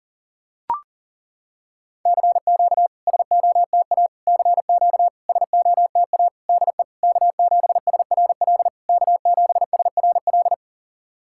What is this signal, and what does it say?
Morse code, heard as a keyed tone, about 40 wpm, 690 Hz. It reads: CQ SOTA CQ SOTA DE K7SRL K7SRL
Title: SOTA QSO Practice 40wpm